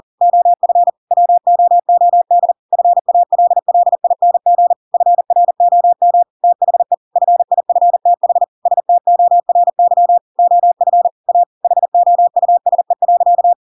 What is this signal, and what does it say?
Morse code, heard as a keyed tone, about 40 wpm, 700 Hz. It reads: OF WOOD FALLING FROM THE FIFTH STORY OF A HOUSE.
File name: Pinocchio / Pinocchio ch7 40 40 ik1hge